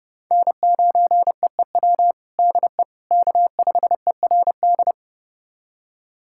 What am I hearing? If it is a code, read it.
Morse code, heard as a keyed tone, about 32 wpm, 700 Hz. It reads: N9EEW DE K5ERD
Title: USA Callsign DE Callsign - 5 Characters - 30wpm